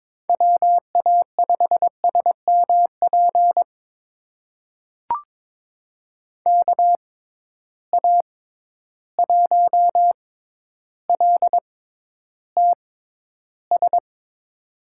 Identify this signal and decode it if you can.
Morse code, heard as a keyed tone, about 22 wpm, spaced as for 11 wpm, 700 Hz. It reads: WA5SMP KA1LTS
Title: USA Call Signs - 6 Characters - 6x Character Spacing - 22wpm